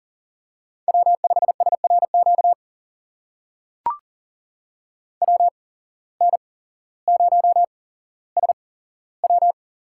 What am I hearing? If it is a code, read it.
Morse code, heard as a keyed tone, about 40 wpm, spaced as for 17 wpm, 700 Hz. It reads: W5SRQ WN0SW
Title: USA Call Signs - 5 Characters - 8x Character Spacing - 40wpm